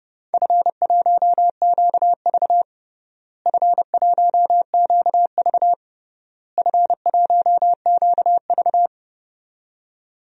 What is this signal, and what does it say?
Morse code, heard as a keyed tone, about 32 wpm, 700 Hz. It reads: F1QV F1QV F1QV
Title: DXCC-100 Call Signs - 4 Characters - Repeated 3x - 30wpm